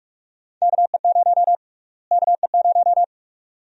Morse code, heard as a keyed tone, about 44 wpm, 700 Hz. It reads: KE0 KE0
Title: USA Call Sign Prefixes - Repeated 2x - 45wpm